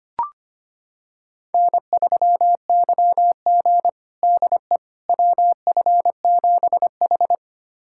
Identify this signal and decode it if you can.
Morse code, heard as a keyed tone, about 25 wpm, 700 Hz. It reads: N3YG DE WF7H
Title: USA Callsign DE Callsign - 4 Characters - 25wpm